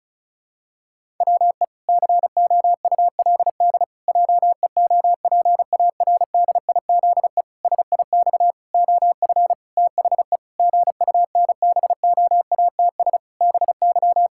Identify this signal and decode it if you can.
Morse code, heard as a keyed tone, about 34 wpm, 700 Hz. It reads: WE COULD JEOPARDIZE SIX OF THE GUNBOATS BY
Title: Pangram Challenge v2 - 35wpm